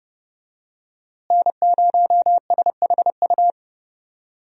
Morse code, heard as a keyed tone, about 31 wpm, 700 Hz. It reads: N0SHU
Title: USA Call Signs - 5 Characters - Repeated 2x - 30wpm